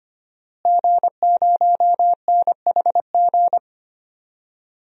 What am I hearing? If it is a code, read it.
Morse code, heard as a keyed tone, about 25 wpm, 700 Hz. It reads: G0NHG
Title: DXCC-100 Call Signs - 5 Characters - Repeated 2x - 25wpm